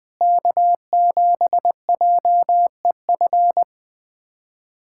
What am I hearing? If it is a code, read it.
Morse code, heard as a keyed tone, about 20 wpm, 700 Hz. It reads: K7JEF
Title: USA Callsign DE Callsign - 5 Characters - 20wpm